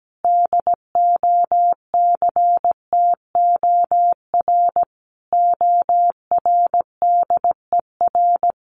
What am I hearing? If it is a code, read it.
Morse code, heard as a keyed tone, about 17 wpm, 700 Hz. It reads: DOCTOR ORDER
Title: Sentences from Top 1500 Words 17wpm